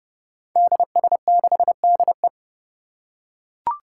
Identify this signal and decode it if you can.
Morse code, heard as a keyed tone, about 30 wpm, 700 Hz. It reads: DS6DE
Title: Five Challenging Letter-Number Combinations 30wpm